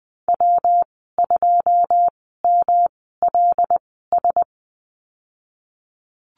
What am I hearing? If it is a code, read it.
Morse code, heard as a keyed tone, about 20 wpm, spaced as for 15 wpm, 700 Hz. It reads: W2MLS
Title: 122 Lesson 20g - Number 3 - Call Signs 20wpm